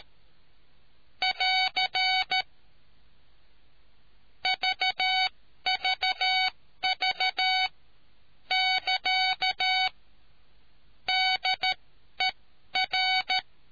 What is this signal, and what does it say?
Morse code, heard as a keyed tone, about 13 wpm, spaced as for 11 wpm, 750 Hz. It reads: <AR> VVV <KA> DER